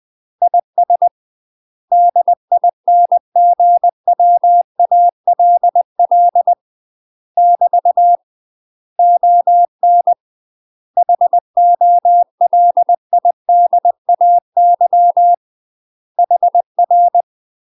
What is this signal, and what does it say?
Morse code, heard as a keyed tone, about 20 wpm, 700 Hz. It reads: IS DINGWALL <BT> ON HOLIDAY HR
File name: UK MASTER QSOs TRAINING / 20WPM / UK MASTER QSOs 0047